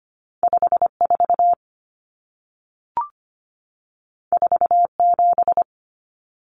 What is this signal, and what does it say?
Morse code, heard as a keyed tone, about 25 wpm, 700 Hz. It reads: 54 47
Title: Numbers - 2-Digit 25wpm